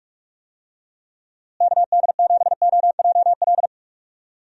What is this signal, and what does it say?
Morse code, heard as a keyed tone, about 46 wpm, 700 Hz. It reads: KD7OJL
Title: USA Call Signs - 6 Characters - Repeated 2x - 45wpm